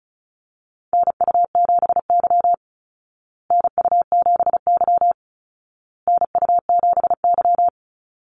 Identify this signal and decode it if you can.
Morse code, heard as a keyed tone, about 34 wpm, 700 Hz. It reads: NU7Y NU7Y NU7Y
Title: Call Signs Repeated 2x - 35wpm